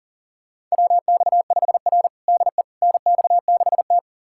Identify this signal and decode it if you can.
Morse code, heard as a keyed tone, about 37 wpm, 700 Hz. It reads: WX5R DE NX6T
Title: USA Callsign DE Callsign - 4 Characters - 40wpm